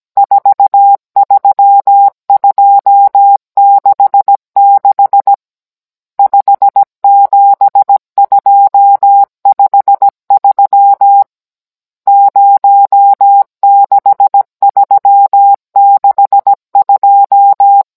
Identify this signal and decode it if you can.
Morse code, heard as a keyed tone, about 17 wpm, 800 Hz. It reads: 43266 57253 06362
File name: Cours-NR18-16mm